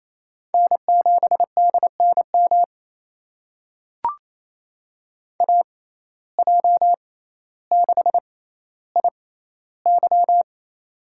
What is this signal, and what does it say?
Morse code, heard as a keyed tone, about 28 wpm, spaced as for 14 wpm, 700 Hz. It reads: N7DNM AJ6IY
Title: USA Call Signs - 5 Characters - 6x Character Spacing - 28wpm